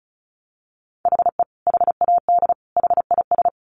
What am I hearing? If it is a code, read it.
Morse code, heard as a keyed tone, about 34 wpm, 700 Hz. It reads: HE HAD HIS
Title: Sentences from Top 900 Words 35wpm